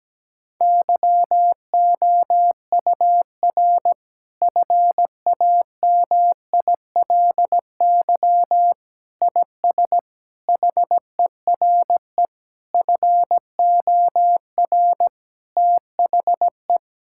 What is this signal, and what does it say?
Morse code, heard as a keyed tone, about 17 wpm, 700 Hz. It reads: YOUR FAMILY IS HERE FOR THE
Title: Sentences from Top 1600 Words 17wpm